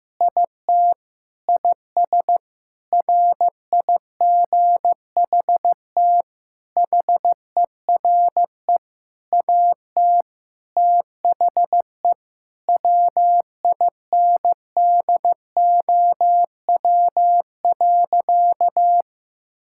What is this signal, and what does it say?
Morse code, heard as a keyed tone, about 15 wpm, 700 Hz. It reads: IT IS RIGHT HERE AT THE WINDOW.
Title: Sentences from Top 1600 Words 15wpm